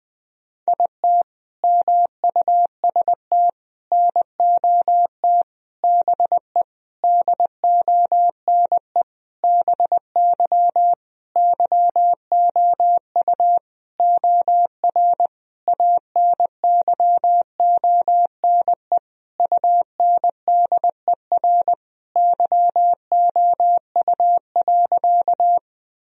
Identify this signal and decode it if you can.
Morse code, heard as a keyed tone, about 20 wpm, 700 Hz. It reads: IT MUST NOT BE DONE BY YOU OR ANYONE UNDER YOU.
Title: Sentences from Top 1000 Words - Review Part 1 - 20wpm